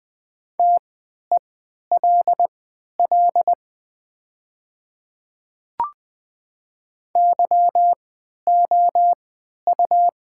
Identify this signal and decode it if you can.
Morse code, heard as a keyed tone, about 20 wpm, spaced as for 11 wpm, 700 Hz. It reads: TELL YOU
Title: 3x Character Spacing - Top 500 Words - 20wpm